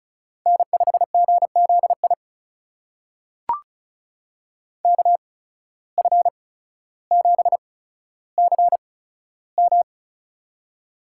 Morse code, heard as a keyed tone, about 34 wpm, spaced as for 15 wpm, 700 Hz. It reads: N5GZI KF7CM
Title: USA Call Signs - 5 Characters - 8x Character Spacing - 35wpm